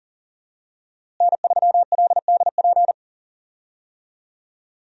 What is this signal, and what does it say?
Morse code, heard as a keyed tone, about 40 wpm, 700 Hz. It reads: N3LDP
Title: USA Call Signs - 5 Characters - Repeated 3x - 40wpm